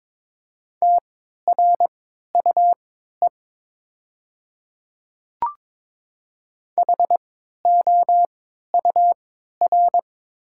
Morse code, heard as a keyed tone, about 22 wpm, spaced as for 13 wpm, 700 Hz. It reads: TRUE HOUR
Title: 3x Character Spacing - Top 500 Words - 22wpm